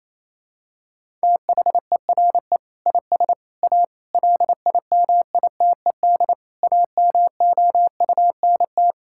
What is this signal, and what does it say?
Morse code, heard as a keyed tone, about 28 wpm, 700 Hz. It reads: THERE IS A LIMITED AMOUNT
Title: Sentences from Top 1000 Words - Review Part 6 - 28wpm